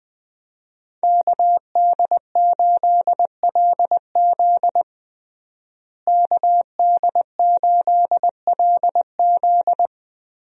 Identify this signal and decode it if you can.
Morse code, heard as a keyed tone, about 20 wpm, 700 Hz. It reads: KD8LZ KD8LZ
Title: USA Call Signs - 5 Characters - Repeated 3x - 20wpm